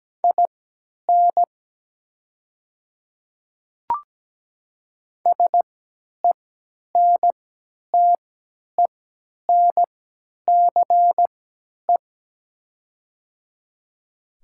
Morse code, heard as a keyed tone, about 17 wpm, spaced as for 10 wpm, 700 Hz. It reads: IN SENTENCE
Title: 3x Character Spacing - Top 200 Words - 17wpm